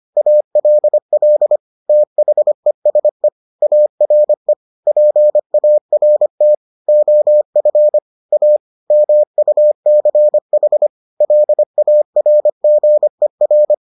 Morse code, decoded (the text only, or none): ALL THESE ARE PART OF A MUCH LARGER